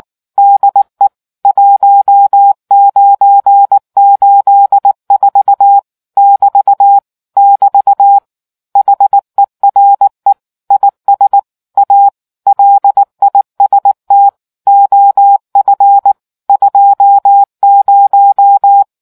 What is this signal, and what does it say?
Morse code, heard as a keyed tone, about 19 wpm, 800 Hz. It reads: DE 1984 <BT> <BT> HERE IS A LIST OF 20